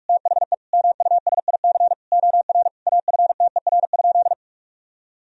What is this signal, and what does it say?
Morse code, heard as a keyed tone, about 41 wpm, 700 Hz. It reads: THE MUSIC OR AFTER?
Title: Sentences from Top 1000 Words - Review Part 1 - 45wpm